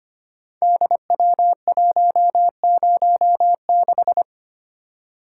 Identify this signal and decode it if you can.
Morse code, heard as a keyed tone, about 25 wpm, 700 Hz. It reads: DW106
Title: Code Groups - 5 Letter:Numbers - 25wpm